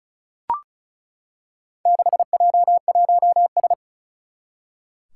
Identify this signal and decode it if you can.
Morse code, heard as a keyed tone, about 36 wpm, 700 Hz. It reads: 6J1S
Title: Four Challenging Letter-Number Combinations 35wpm